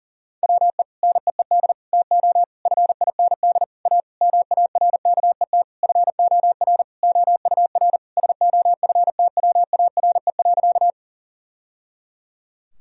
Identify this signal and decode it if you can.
Morse code, heard as a keyed tone, about 40 wpm, 700 Hz. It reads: WE NEED TO FIND A MARKET FOR OUR SOFTWARE.
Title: Sentences from Top 1000 Words - Review Part 3 - 40wpm